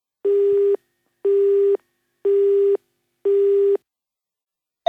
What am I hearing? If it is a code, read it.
There is no code.